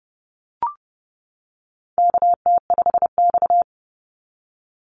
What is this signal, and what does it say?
Morse code, heard as a keyed tone, about 30 wpm, 700 Hz. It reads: KT5X
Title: NA SOTA Activator Callsigns 30wpm